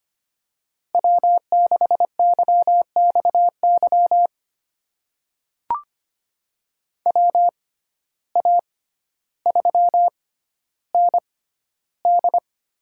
Morse code, heard as a keyed tone, about 25 wpm, spaced as for 13 wpm, 700 Hz. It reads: W6YXY WA3ND
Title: USA Call Signs - 6x Character Spacing - 25wpm